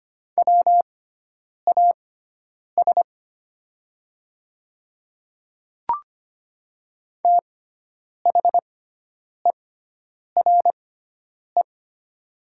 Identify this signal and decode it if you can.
Morse code, heard as a keyed tone, about 25 wpm, spaced as for 9 wpm, 700 Hz. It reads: WAS THERE
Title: 6x Character Spacing - Top 100 Words - 25wpm